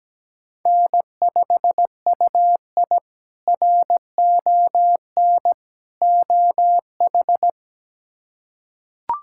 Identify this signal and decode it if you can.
Morse code, heard as a keyed tone, about 17 wpm, 700 Hz. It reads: N5UI RON OH
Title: K1USN SST QSO Practice - 17wpm